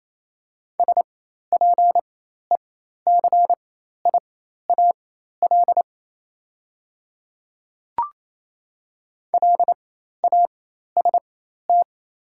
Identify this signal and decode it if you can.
Morse code, heard as a keyed tone, about 27 wpm, spaced as for 13 wpm, 700 Hz. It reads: SPECIAL LAST
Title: 4x Character Spacing - Top 500 Words - 28wpm